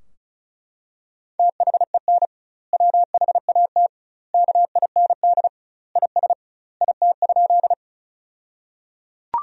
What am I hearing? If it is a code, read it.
Morse code, heard as a keyed tone, about 34 wpm, 700 Hz. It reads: THEN WHAT KIND IS IT?
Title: 2x Word-Spacing - Sentences from Top 200 Words - 35wpm